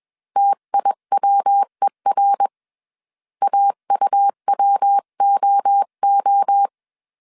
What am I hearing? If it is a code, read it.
Morse code, heard as a keyed tone, about 21 wpm, 800 Hz. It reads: TIWER AUWOO
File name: 20wpm